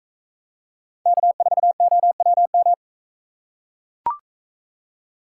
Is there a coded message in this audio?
Yes: Morse code from 1.1 to 2.7 s, heard as a keyed tone, about 41 wpm, 700 Hz. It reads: K4OWM